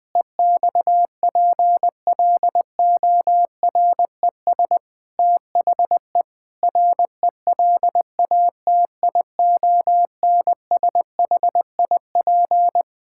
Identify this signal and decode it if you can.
Morse code, heard as a keyed tone, about 20 wpm, 700 Hz. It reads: EXPLORES THE RELATIONSHIP
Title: Sentences with 6-Word Phrases - 20wpm